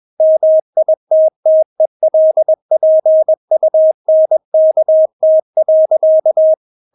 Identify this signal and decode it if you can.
Morse code, heard as a keyed tone, about 21 wpm, 630 Hz. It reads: MITTELPUNKT.
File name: Witz2 21wpm